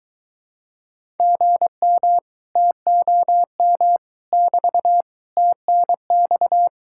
Morse code, heard as a keyed tone, about 23 wpm, 700 Hz. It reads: GM TOM <BT> TNX